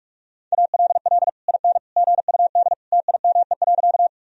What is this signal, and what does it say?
Morse code, heard as a keyed tone, about 41 wpm, 700 Hz. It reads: ALL IN GUD TIME.
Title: Sentences from Top 200 Words and Abbreviations 45wpm